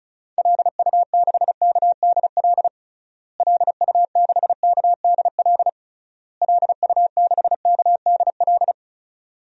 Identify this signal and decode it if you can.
Morse code, heard as a keyed tone, about 36 wpm, 700 Hz. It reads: LU6KDL LU6KDL LU6KDL
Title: DXCC-100 Call Signs - 6 Characters - Repeated 3x - 35wpm